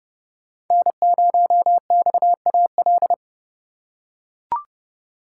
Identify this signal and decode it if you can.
Morse code, heard as a keyed tone, about 29 wpm, 700 Hz. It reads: N0XAL